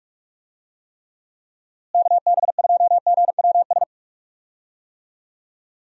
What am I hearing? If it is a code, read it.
Morse code, heard as a keyed tone, about 46 wpm, 700 Hz. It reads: KB2GWS
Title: USA Call Signs - 6 Characters - Repeated 2x - 45wpm